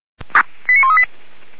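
Five-tone selective call (ZVEI), heard as quick runs of five tones, tones of about 80 ms. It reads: 89138
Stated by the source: ZVEI five-tone selective call 89132 received by SDR, from the Feuerwehr Orlamünde audio server.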